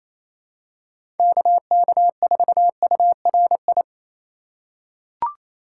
Morse code, heard as a keyed tone, about 28 wpm, 700 Hz. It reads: KK4URI
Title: USA Call Signs - 4x Character Spacing - 28wpm